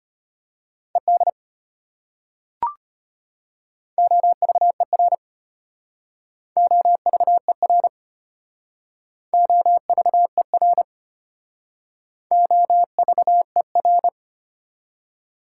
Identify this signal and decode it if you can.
Morse code, heard as a keyed tone, about 33 wpm, 700 Hz. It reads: ED OVER OVER OVER OVER